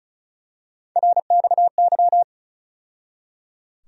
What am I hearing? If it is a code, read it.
Morse code, heard as a keyed tone, about 36 wpm, 700 Hz. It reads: RXY
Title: USA Call Sign Postfixes - 3 Letters 35wpm